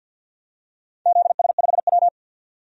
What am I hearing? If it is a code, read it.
Morse code, heard as a keyed tone, about 51 wpm, 700 Hz. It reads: ZS5W
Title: DXCC-100 Call Signs - 4 Characters - 50wpm